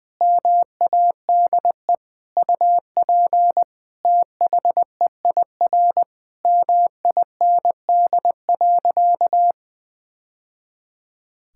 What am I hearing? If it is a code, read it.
Morse code, heard as a keyed tone, about 20 wpm, 700 Hz. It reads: MADE UP THEIR MIND.
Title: Sentences from Top 1000 Words - Review Part 5 - 20wpm